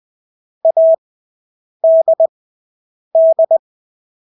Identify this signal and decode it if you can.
Morse code, heard as a keyed tone, about 20 wpm, spaced as for 8 wpm, 650 Hz. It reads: ADD
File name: Engelske ord 20/8 wpm